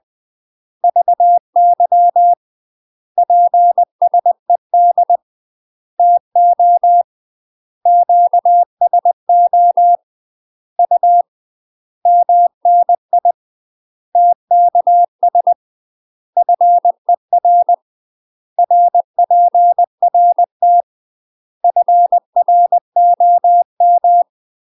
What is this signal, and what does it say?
Morse code, heard as a keyed tone, about 20 wpm, 700 Hz. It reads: VY PSED TO QSO U MNI TKS FER RPRT FROM
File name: UK MASTER QSOs TRAINING / 20WPM / UK MASTER QSOs 0044